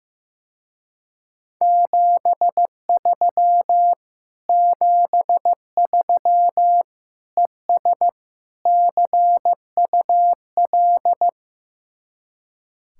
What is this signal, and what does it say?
Morse code, heard as a keyed tone, about 16 wpm, 700 Hz. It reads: 73 73 ES CUL
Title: Speed-Racing Common QSO Phrases Encore - 15wpm